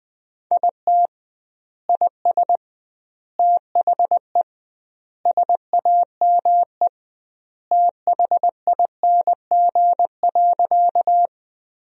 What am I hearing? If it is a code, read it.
Morse code, heard as a keyed tone, about 20 wpm, 700 Hz. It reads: IT IS THE SAME THING.
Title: 2x Word-Spacing - Sentences from Top 200 Words - 20wpm